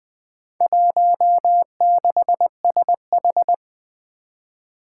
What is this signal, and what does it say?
Morse code, heard as a keyed tone, about 20 wpm, 700 Hz. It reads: 16SH
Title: Four Challenging Letter-Number Combinations 20wpm